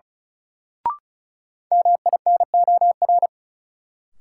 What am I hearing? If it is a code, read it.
Morse code, heard as a keyed tone, about 36 wpm, 700 Hz. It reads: MINOR
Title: Sentences from Top 1500 Words - New Word Drill Alt - 35wpm